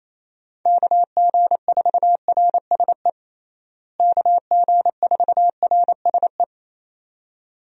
Morse code, heard as a keyed tone, about 28 wpm, 700 Hz. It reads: KG4RSE KG4RSE
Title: USA Call Signs - 6 Characters - Repeated 2x - 28wpm